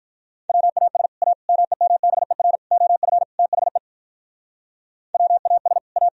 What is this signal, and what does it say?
Morse code, heard as a keyed tone, about 46 wpm, 700 Hz. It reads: WAS A MEMBER OF THE WAS A